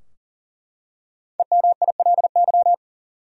Morse code, heard as a keyed tone, about 38 wpm, 700 Hz. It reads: EMILY
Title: USA Common Names - Max 5 Letters - 40wpm